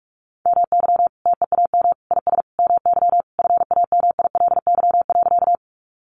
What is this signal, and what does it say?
Morse code, heard as a keyed tone, about 44 wpm, 700 Hz. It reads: MY TEAM IS MY FAMILY.
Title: Sentences from Top 1500 Words 45wpm